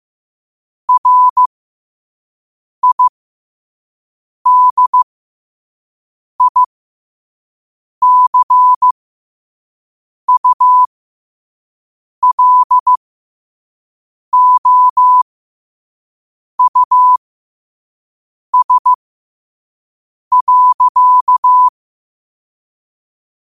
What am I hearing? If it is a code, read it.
Morse code, heard as a keyed tone, about 15 wpm, spaced as for 5 wpm, 1000 Hz. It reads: RIDICULOUS.
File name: qotdfiles / 5wpm